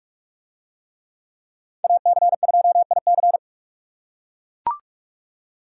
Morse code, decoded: AC2IC

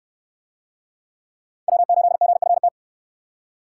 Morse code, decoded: W8GFT

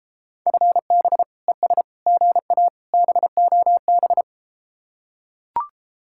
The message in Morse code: FB ES GA BOB

